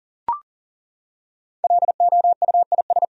Morse code, heard as a keyed tone, about 37 wpm, 700 Hz. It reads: LOUIS
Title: USA Common Names - Max 5 Letters - 40wpm